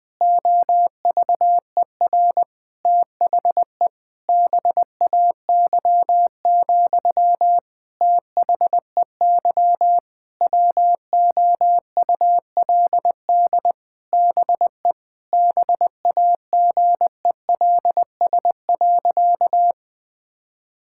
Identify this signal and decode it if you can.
Morse code, heard as a keyed tone, about 20 wpm, 700 Hz. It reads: OVER THE BAY, THEY WOULD BE BAGELS.
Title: All-Age Appropriate Jokes - 20wpm